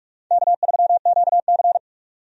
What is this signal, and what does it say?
Morse code, heard as a keyed tone, about 46 wpm, 700 Hz. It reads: K3QC
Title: USA Call Signs - 4 Characters - 45wpm